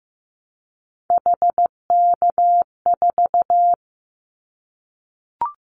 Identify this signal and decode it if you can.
Morse code, heard as a keyed tone, about 15 wpm, 700 Hz. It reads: HK4
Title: Callsign Prefixes for Top 100 DXCC - 15wpm